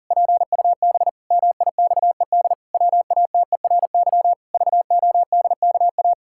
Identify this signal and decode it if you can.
Morse code, heard as a keyed tone, about 40 wpm, 700 Hz. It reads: PUB MIXED WATERY VODKA
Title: Pangram Challenge v2 - 40wpm